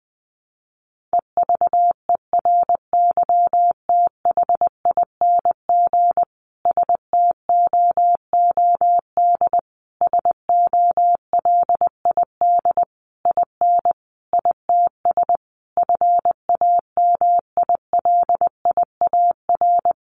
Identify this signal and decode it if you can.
Morse code, heard as a keyed tone, about 20 wpm, 700 Hz. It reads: EVERYTHING STOOD SOLID IN ITS FAMILIAR
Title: Sentences from Top 1500 Words 20wpm